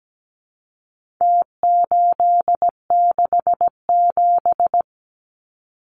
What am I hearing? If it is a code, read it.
Morse code, heard as a keyed tone, about 17 wpm, 700 Hz. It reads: T867
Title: Numbers - Contest Style - 4-Digit 17wpm